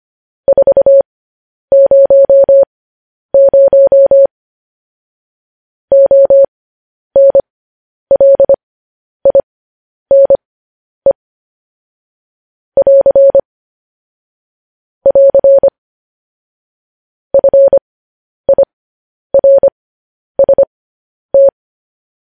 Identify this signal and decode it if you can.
Morse code, heard as a keyed tone, about 25 wpm, spaced as for 10 wpm, 550 Hz. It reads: 400 ONLINE <AR> <AR> FIRST
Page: News Headlines in Morse Code at 10 WPM